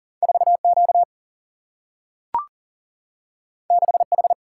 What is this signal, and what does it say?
Morse code, heard as a keyed tone, about 39 wpm, 700 Hz. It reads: 4Q 6H
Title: Single-Letter Single-Number 40wpm